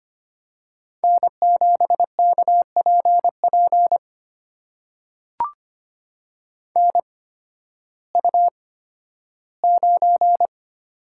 Morse code, heard as a keyed tone, about 25 wpm, spaced as for 13 wpm, 700 Hz. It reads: N7KPP NU9
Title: USA Call Signs - 8x Character Spacing - 25wpm